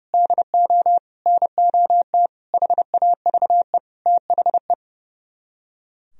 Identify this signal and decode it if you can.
Morse code, heard as a keyed tone, about 38 wpm, 700 Hz. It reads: DO NOT HAVE THE